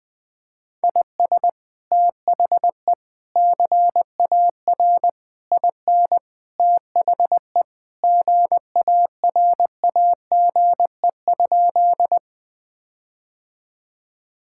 Morse code, heard as a keyed tone, about 20 wpm, 700 Hz. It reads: IS THE CAR IN THE GARAGE?
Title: Sentences from Top 1600 Words 20wpm